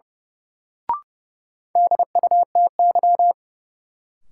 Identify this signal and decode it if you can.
Morse code, heard as a keyed tone, about 30 wpm, 700 Hz. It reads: DUTY
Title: Sentences from Top 1500 Words - New Word Drill - 30wpm